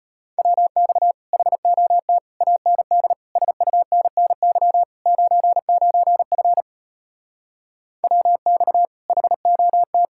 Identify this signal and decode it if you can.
Morse code, heard as a keyed tone, about 33 wpm, 700 Hz. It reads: WX HOT AND SUNNY 99F WX HOT